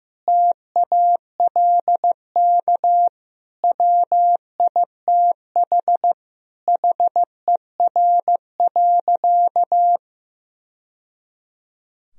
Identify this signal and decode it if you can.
Morse code, heard as a keyed tone, about 15 wpm, 700 Hz. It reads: TALK WITH HER.
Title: Sentences from Top 1000 Words - Review Part 4 - 15wpm